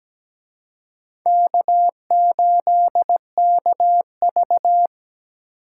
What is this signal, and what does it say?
Morse code, heard as a keyed tone, about 20 wpm, 700 Hz. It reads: K8KV
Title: Speed Racing - USA 1x2 Call Signs 17wpm